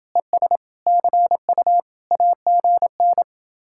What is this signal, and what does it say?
Morse code, heard as a keyed tone, about 28 wpm, 700 Hz. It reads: ES CU AGN